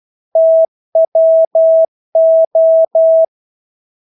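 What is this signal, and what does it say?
Morse code, heard as a keyed tone, about 12 wpm, 650 Hz. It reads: TWO
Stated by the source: Morse practice of English words at 12 wpm.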